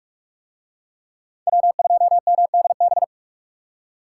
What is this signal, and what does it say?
Morse code, heard as a keyed tone, about 45 wpm, 700 Hz. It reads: W2MDB